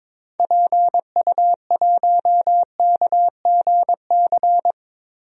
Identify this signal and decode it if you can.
Morse code, heard as a keyed tone, about 22 wpm, 700 Hz. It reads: PU1KGC
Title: DXCC-100 Call Signs - 6 Characters - Repeated 3x - 22wpm